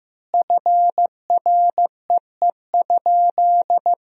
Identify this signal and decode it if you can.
Morse code, heard as a keyed tone, about 15 wpm, 700 Hz. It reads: FREE?
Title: 3x Word-Spacing - Sentences from Top 500 Words - 15wpm